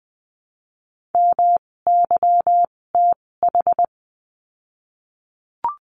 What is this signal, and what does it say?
Morse code, heard as a keyed tone, about 20 wpm, spaced as for 16 wpm, 700 Hz. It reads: MYTH
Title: 256 Lesson 42a - Words 12-20wpm